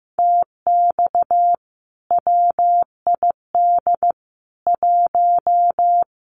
Morse code, heard as a keyed tone, about 14 wpm, 700 Hz. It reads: TX WID 1
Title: Common QSO Phrases 15wpm